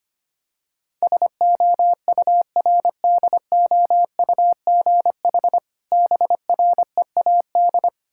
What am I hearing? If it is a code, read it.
Morse code, heard as a keyed tone, about 25 wpm, 700 Hz. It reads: SOURDOUGH BREAD
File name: practice / 25wpm / Difficult Words with Hints - 25wpm